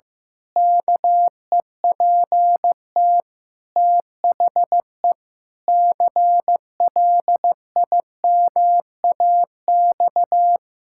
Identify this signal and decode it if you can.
Morse code, heard as a keyed tone, about 15 wpm, 700 Hz. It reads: KEPT THE CLIMAX
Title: Pangram Challenge v2 - 15wpm